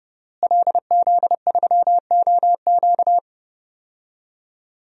Morse code, heard as a keyed tone, about 30 wpm, 700 Hz. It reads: LZ3OQ
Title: DXCC-100 Call Signs - 5 Characters - Repeated 3x - 30wpm